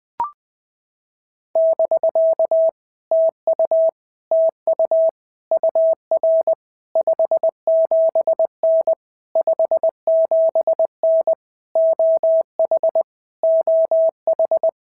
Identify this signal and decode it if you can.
Morse code, heard as a keyed tone, about 21 wpm, 650 Hz. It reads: <BK> TU TU UR 57N 57N OH OH
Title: POTA QSO Practice - 20wpm